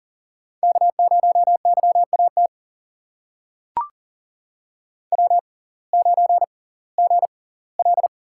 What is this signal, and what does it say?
Morse code, heard as a keyed tone, about 38 wpm, spaced as for 22 wpm, 700 Hz. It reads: K0YAT W9GL
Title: USA Call Signs - 5 Characters - 6x Character Spacing - 40wpm